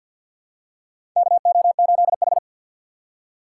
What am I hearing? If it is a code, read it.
Morse code, heard as a keyed tone, about 49 wpm, 700 Hz. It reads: KO8H
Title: USA Call Signs - 4 Characters - 6x Character Spacing - 50wpm